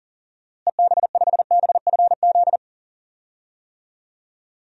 Morse code, heard as a keyed tone, about 40 wpm, 700 Hz. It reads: EB5BFZ
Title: DXCC-100 Call Signs - 6 Characters - Repeated 2x - 40wpm